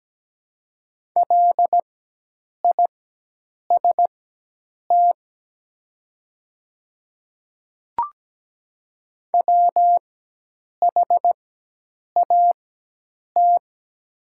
Morse code, heard as a keyed tone, about 17 wpm, spaced as for 8 wpm, 700 Hz. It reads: LIST WHAT
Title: 4x Character Spacing - Top 500 Words - 17wpm